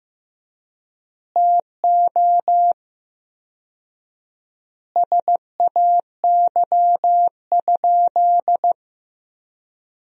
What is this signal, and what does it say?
Morse code, heard as a keyed tone, about 15 wpm, 700 Hz. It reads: TO SAY?
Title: 4x Word-Spacing - Sentences from Top 200 Words - 15wpm